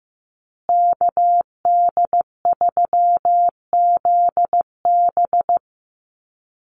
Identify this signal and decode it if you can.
Morse code, heard as a keyed tone, about 15 wpm, 700 Hz. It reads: KD3ZB
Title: USA Call Signs Repeated 2x 15wpm